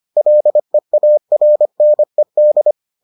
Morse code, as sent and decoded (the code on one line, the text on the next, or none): .-.. .- .-. -. -..
LEARNED